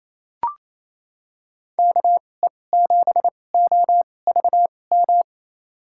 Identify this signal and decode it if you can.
Morse code, heard as a keyed tone, about 28 wpm, spaced as for 20 wpm, 700 Hz. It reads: KE7OVM
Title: USA Call Signs - 6 Characters - 2x Character Spacing - 28wpm